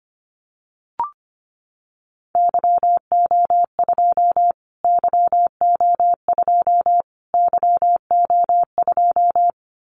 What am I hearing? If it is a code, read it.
Morse code, heard as a keyed tone, about 25 wpm, 700 Hz. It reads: YO2 YO2 YO2
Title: Callsign Prefixes for Top 25 DXCC - Repeated 3x - 25wpm